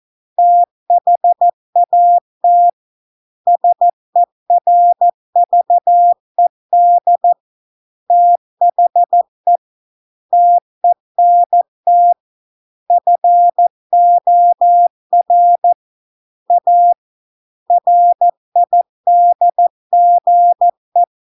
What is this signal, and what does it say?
Morse code, heard as a keyed tone, about 14 wpm, 700 Hz. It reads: THAT SERVED THE TENT FOR A RIDGE